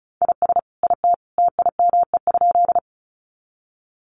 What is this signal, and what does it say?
Morse code, heard as a keyed tone, about 34 wpm, 700 Hz. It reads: IS IT TIME?
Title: Sentences from Top 100 Words 35wpm